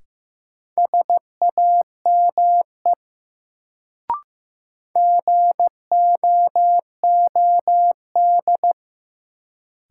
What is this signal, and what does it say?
Morse code, heard as a keyed tone, about 15 wpm, 700 Hz. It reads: SAME GOOD
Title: Sets of 2 Words - New Word Drill - 15wpm